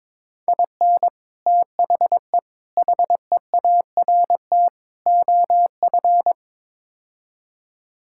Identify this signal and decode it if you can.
Morse code, heard as a keyed tone, about 22 wpm, 700 Hz. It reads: IN THE HEART OF
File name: practice / 22wpm / Build A Phrase - 5-Word Phrases - 22wpm